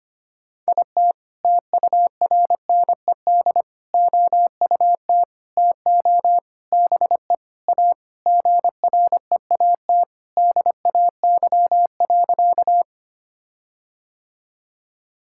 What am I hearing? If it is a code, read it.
Morse code, heard as a keyed tone, about 25 wpm, 700 Hz. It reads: IT TURNED OUT TO BE A GREAT DAY.
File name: practice / 25wpm / Sentences with 4-Word Phrases - 25wpm